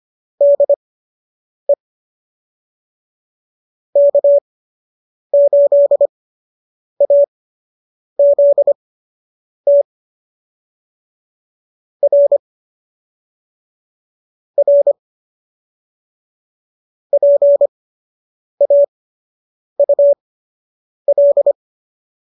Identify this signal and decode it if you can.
Morse code, heard as a keyed tone, about 25 wpm, spaced as for 8 wpm, 590 Hz. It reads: DE K8AZT R R PAUL
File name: qso006 08